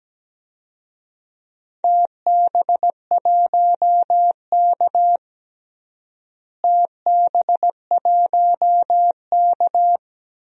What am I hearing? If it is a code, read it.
Morse code, heard as a keyed tone, about 17 wpm, 700 Hz. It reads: TB1K TB1K
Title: DXCC-100 Call Signs - 4 Characters - Repeated 2x - 17wpm